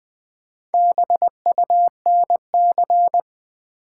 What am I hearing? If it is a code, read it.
Morse code, heard as a keyed tone, about 20 wpm, 700 Hz. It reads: BUNC
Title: Build A Word - 1 Letter At A Time - 5-Letter Words - 20wpm